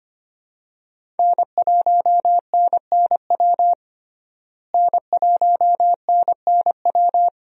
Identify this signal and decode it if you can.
Morse code, heard as a keyed tone, about 25 wpm, 700 Hz. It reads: N1NNW N1NNW
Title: USA Call Signs - 5 Characters - Repeated 3x - 25wpm